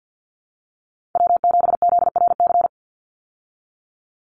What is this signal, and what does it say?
Morse code, heard as a keyed tone, about 48 wpm, 700 Hz. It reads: W7ZRC
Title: Call Signs Repeated 3x - 50wpm